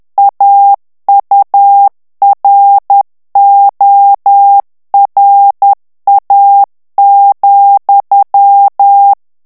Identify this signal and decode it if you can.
Morse code, heard as a keyed tone, about 10 wpm, 800 Hz. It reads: AURORA,
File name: ANGLAIS texte clair 3 vitesse 10 mots minute